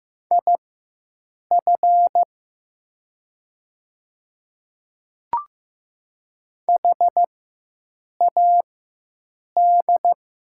Morse code, heard as a keyed tone, about 15 wpm, spaced as for 7 wpm, 700 Hz. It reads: IF HAD
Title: 4x Character Spacing - Top 100 Words - 15wpm